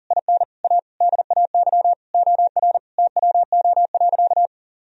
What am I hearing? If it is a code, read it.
Morse code, heard as a keyed tone, about 40 wpm, 700 Hz. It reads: IN A DAY OR TWO.
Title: Sentences from Top 1000 Words - Review Part 1 - 40wpm